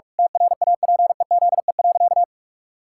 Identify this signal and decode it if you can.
Morse code, heard as a keyed tone, about 46 wpm, 700 Hz. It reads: TRAPEZE.
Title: Pangram Challenge v2 - 45wpm